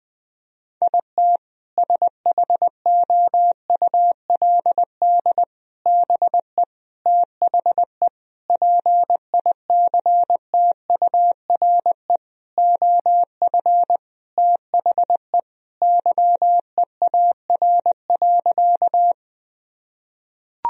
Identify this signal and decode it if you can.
Morse code, heard as a keyed tone, about 20 wpm, 700 Hz. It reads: IT SHOULD BE THE PICTURE OF THE YEAR.
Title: Sentences from Top 1000 Words - Review Part 1 - 20wpm